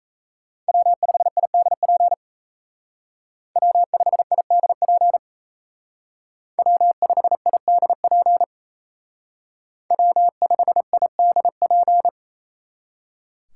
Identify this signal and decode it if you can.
Morse code, heard as a keyed tone, about 33 wpm, 700 Hz. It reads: W5IDP W5IDP W5IDP W5IDP
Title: Speed Racing - USA Call Signs 28wpm